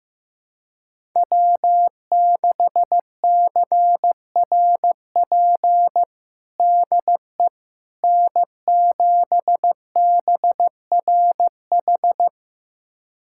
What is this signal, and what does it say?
Morse code, heard as a keyed tone, about 15 wpm, 700 Hz. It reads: W6CRP DE N7BRH
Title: USA Callsign DE Callsign - 5 Characters - 15wpm